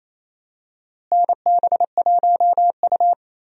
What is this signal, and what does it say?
Morse code, heard as a keyed tone, about 28 wpm, 700 Hz. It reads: NB1U
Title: USA Call Signs - 4 Characters - Repeated 2x - 28wpm